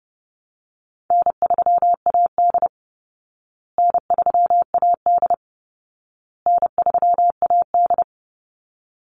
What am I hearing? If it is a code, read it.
Morse code, heard as a keyed tone, about 30 wpm, 700 Hz. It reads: N3AD N3AD N3AD
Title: Call Signs Repeated 2x - 30wpm